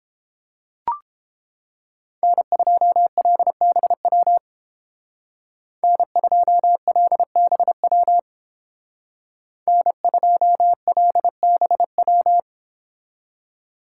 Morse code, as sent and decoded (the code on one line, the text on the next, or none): -. ..--- .-.. -... .-- / -. ..--- .-.. -... .-- / -. ..--- .-.. -... .--
N2LBW N2LBW N2LBW